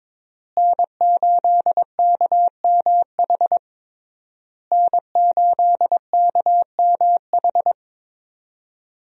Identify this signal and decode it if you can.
Morse code, heard as a keyed tone, about 22 wpm, 700 Hz. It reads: N8KMH N8KMH
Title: USA Call Signs - 5 Characters - Repeated 3x - 22wpm